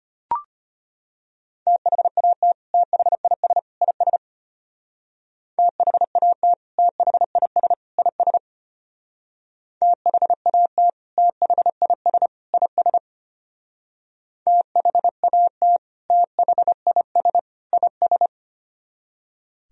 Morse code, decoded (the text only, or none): THAT THIS IS THAT THIS IS THAT THIS IS THAT THIS IS